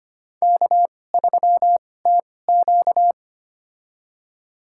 Morse code, heard as a keyed tone, about 25 wpm, spaced as for 18 wpm, 700 Hz. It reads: K3TQ
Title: USA Call Signs - 4 Characters - 2x Character Spacing - 25wpm